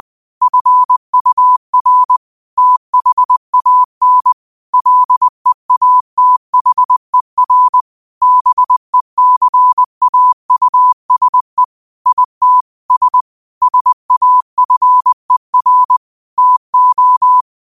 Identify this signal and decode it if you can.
Morse code, heard as a keyed tone, about 20 wpm, 1000 Hz. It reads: FUR THAN LEATHER BECAUSE IT S SAFER TO